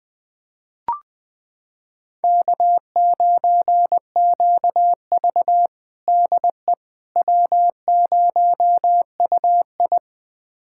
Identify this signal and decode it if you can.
Morse code, heard as a keyed tone, about 20 wpm, 700 Hz. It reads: K9QV DE W0UI
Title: USA Callsign DE Callsign - 4 Characters - 20wpm